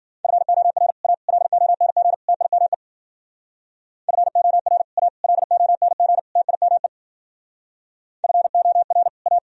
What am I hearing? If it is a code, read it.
Morse code, heard as a keyed tone, about 51 wpm, 700 Hz. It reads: FOR A LONG TIME FOR A LONG TIME FOR A